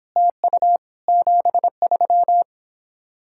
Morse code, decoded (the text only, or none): TU 73